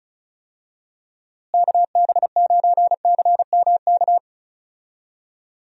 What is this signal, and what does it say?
Morse code, heard as a keyed tone, about 34 wpm, 700 Hz. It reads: KB9CMK
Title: USA Call Signs - 6 Characters - Repeated 2x - 35wpm